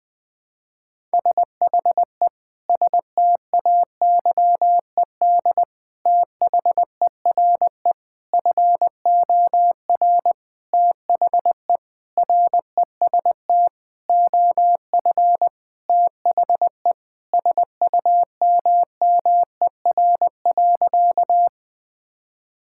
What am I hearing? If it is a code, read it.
Morse code, heard as a keyed tone, about 21 wpm, 700 Hz. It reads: SHE STAYED THERE FOR THE REST OF THE SUMMER.
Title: Sentences with 5-Word Phrases - 20wpm